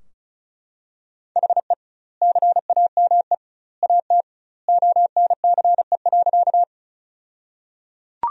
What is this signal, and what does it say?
Morse code, heard as a keyed tone, about 34 wpm, 700 Hz. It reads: HE CAME AT ONCE.
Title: 2x Word-Spacing - Sentences from Top 300 Words - 35wpm